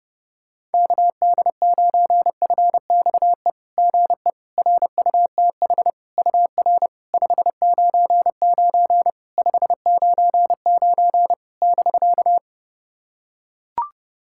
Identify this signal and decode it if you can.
Morse code, heard as a keyed tone, about 28 wpm, 690 Hz. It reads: KD9FXE GE RUTH UR 599 599 <BK>
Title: SOTA QSO Practice 30wpm